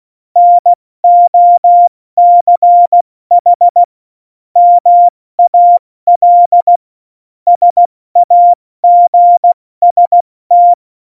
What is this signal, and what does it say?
Morse code, heard as a keyed tone, about 16 wpm, 700 Hz. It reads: NOCH MAL SAGST
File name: Morsekurs / Witze / Witz 11. September 2018, 70 F80